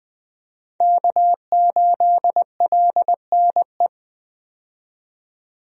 Morse code, heard as a keyed tone, about 21 wpm, 700 Hz. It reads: K8LNE
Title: USA Call Signs - 5 Characters - 20wpm